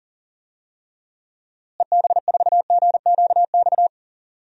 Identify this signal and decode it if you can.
Morse code, heard as a keyed tone, about 40 wpm, 700 Hz. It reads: EB4GQX
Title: DXCC-100 Call Signs - 6 Characters - Repeated 2x - 40wpm